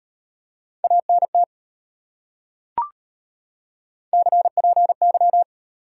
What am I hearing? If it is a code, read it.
Morse code, heard as a keyed tone, about 31 wpm, 700 Hz. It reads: ANT CPY
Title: Speed-Racing Top 180 QSO Elements 25wpm